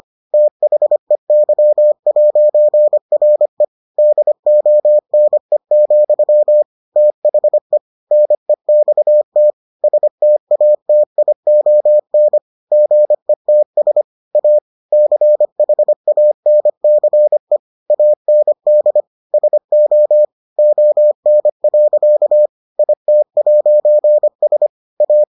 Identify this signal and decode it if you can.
Morse code, heard as a keyed tone, about 25 wpm, 600 Hz. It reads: THEY'RE DONE, THE NEXT STATION GETS A CHANCE AND SO ON. IT'S A